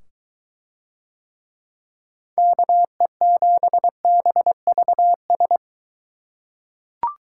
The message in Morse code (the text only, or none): KE7BVS